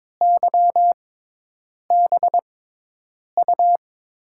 Morse code, heard as a keyed tone, about 22 wpm, spaced as for 8 wpm, 700 Hz. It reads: YBU
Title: USA Call Signs - 5 Characters - 6x Character Spacing - 22wpm